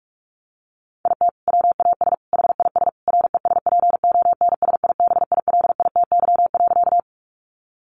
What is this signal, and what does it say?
Morse code, heard as a keyed tone, about 44 wpm, 700 Hz. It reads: IT WAS HIS RESPONSIBILITY.
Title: Sentences from Top 1200 Words 45wpm